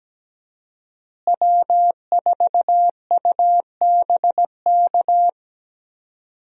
Morse code, heard as a keyed tone, about 17 wpm, 700 Hz. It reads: W4UBK
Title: USA Call Signs - 5 Characters - Repeated 3x - 17wpm